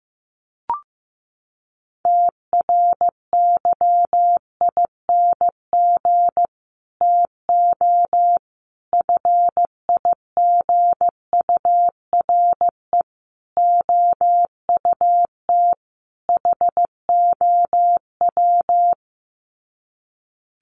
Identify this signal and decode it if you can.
Morse code, heard as a keyed tone, about 15 wpm, 700 Hz. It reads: TRYING TO FIGURE OUT HOW
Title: Sets of 5 Words Encore 15wpm